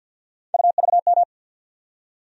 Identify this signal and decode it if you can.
Morse code, heard as a keyed tone, about 53 wpm, 700 Hz. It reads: UVM
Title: USA Call Sign Postfixes - 3 Letters 50wpm